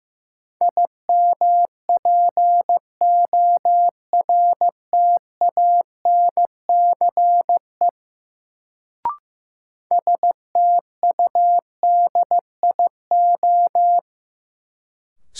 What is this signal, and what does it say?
Morse code, heard as a keyed tone, about 15 wpm, 700 Hz. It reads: IMPORTANCE STUDIO
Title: Sentences from Top 1200 Words - New Word Drill Alt - 15wpm